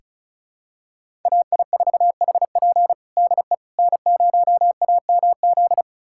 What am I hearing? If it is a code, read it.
Morse code, heard as a keyed tone, about 34 wpm, 700 Hz. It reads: AI4HP DE N0AMZ
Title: USA Callsign DE Callsign - 5 Characters - 35wpm